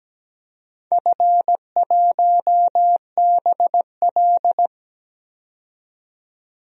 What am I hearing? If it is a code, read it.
Morse code, heard as a keyed tone, about 17 wpm, 700 Hz. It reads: F1BL
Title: DXCC-100 Call Signs - 4 Characters - 17wpm